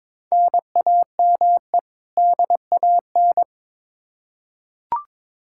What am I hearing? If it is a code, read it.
Morse code, heard as a keyed tone, about 22 wpm, 700 Hz. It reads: NAME DAN
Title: Practice QSO - Exchange 1 - Beginning - 22wpm